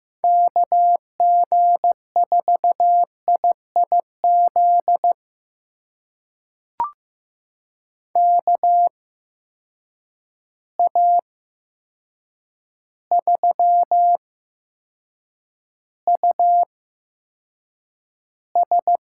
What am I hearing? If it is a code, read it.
Morse code, heard as a keyed tone, about 15 wpm, spaced as for 7 wpm, 700 Hz. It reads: KG4IIZ KA3US
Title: USA Call Signs - 6 Characters - 8x Character Spacing - 15wpm